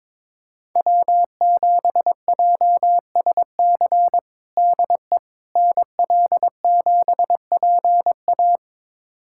Morse code, heard as a keyed tone, about 22 wpm, 700 Hz. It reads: W7JSC DE NL7PA
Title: USA Callsign DE Callsign - 5 Characters - 22wpm